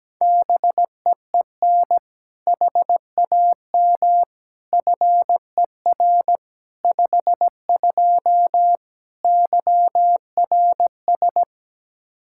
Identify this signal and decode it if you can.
Morse code, heard as a keyed tone, about 17 wpm, 700 Hz. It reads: BEEN HAM FER 52 YRS
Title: Practice QSO - Exchange 3 - More Info - 17wpm